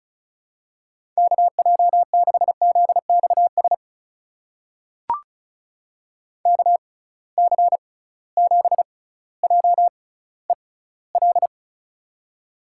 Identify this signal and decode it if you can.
Morse code, heard as a keyed tone, about 36 wpm, spaced as for 18 wpm, 700 Hz. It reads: KJ6ZXS KC7JEL
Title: USA Call Signs - 6 Characters - 6x Character Spacing - 35wpm